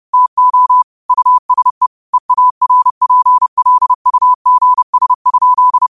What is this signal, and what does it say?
Morse code, heard as a keyed tone, about 30 wpm, 1000 Hz. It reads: TO USE EARPLUGS?
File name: qotdfiles / 30wpm